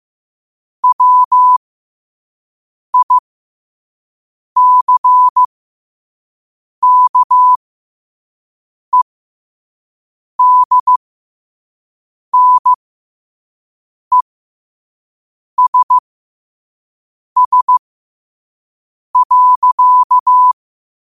Morse code, heard as a keyed tone, about 15 wpm, spaced as for 5 wpm, 1000 Hz. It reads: WICKEDNESS.
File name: 5wpm